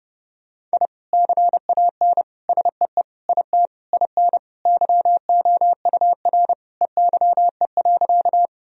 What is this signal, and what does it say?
Morse code, heard as a keyed tone, about 30 wpm, 700 Hz. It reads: I CAN SEE IT IN YOUR EYE.
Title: Sentences from Top 1000 Words - Review Part 1 - 30wpm